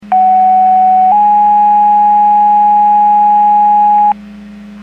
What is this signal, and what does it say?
Two-tone page, heard as a one-second tone, then a three-second tone, about 750 then 830 Hz. It is heard from 0.1 to 4.1 s.
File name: Portage / 01102 15-41 Portage 154.385 Paging